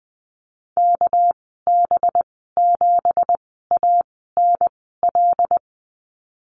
Morse code, heard as a keyed tone, about 20 wpm, spaced as for 14 wpm, 700 Hz. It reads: KB7ANL
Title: 245 Lesson 39g - Letter z - Call Signs 20wpm